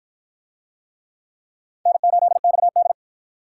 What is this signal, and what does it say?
Morse code, heard as a keyed tone, about 53 wpm, 700 Hz. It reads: N8XD